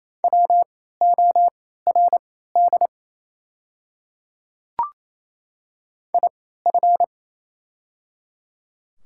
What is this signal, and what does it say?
Morse code, heard as a keyed tone, about 28 wpm, spaced as for 16 wpm, 700 Hz. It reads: WORD IF
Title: 3x Character Spacing - Top 100 Words - 28wpm